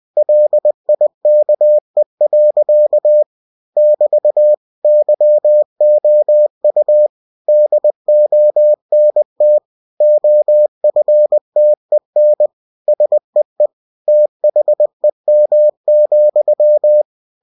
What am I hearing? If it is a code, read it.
Morse code, heard as a keyed tone, about 20 wpm, 600 Hz. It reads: LIKE. <BT> YOU DONT OFTEN SEE THEM,
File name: cw-books / winnie-the-pooh / winnie-20wpm-page-0033